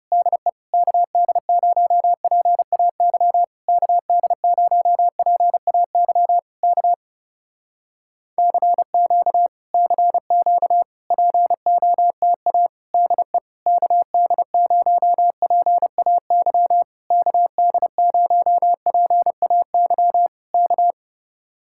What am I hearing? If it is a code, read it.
Morse code, heard as a keyed tone, about 30 wpm, 700 Hz. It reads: DE KD0PAY KD0PAY K CQ CQ POTA DE KD0PAY KD0PAY K
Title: Speed-Racing Common QSO Phrases Encore - 30wpm